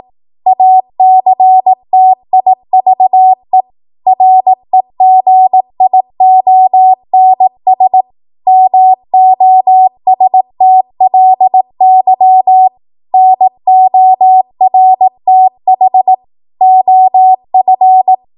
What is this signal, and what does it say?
Morse code, heard as a keyed tone, about 18 wpm, 750 Hz. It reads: ACTIVE REGIONS MOSTLY NORTH OF
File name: Morse / 18WPM